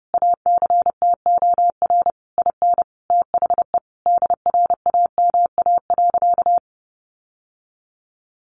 Morse code, heard as a keyed tone, about 30 wpm, 700 Hz. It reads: ACTOR IN THE DRAMA.
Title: Sentences from Top 1500 Words 30wpm